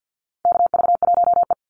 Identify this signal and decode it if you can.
Morse code, heard as a keyed tone, about 50 wpm, 700 Hz. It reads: KV1E
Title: Call Signs - 50wpm